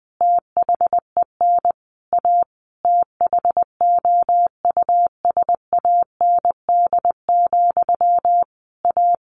Morse code, heard as a keyed tone, about 20 wpm, 700 Hz. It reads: THEN A THOUSAND, A
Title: A Sound Of Thunder 20wpm